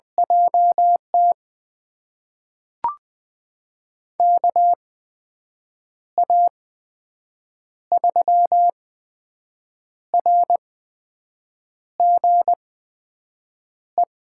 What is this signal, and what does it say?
Morse code, heard as a keyed tone, about 21 wpm, spaced as for 6 wpm, 700 Hz. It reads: JT KA3RGE
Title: USA Call Signs - 6 Characters - 8x Character Spacing - 20wpm